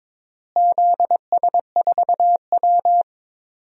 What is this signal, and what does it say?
Morse code, heard as a keyed tone, about 22 wpm, 700 Hz. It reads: ZS4W
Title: DXCC-100 Call Signs - 4 Characters - Repeated 2x - 22wpm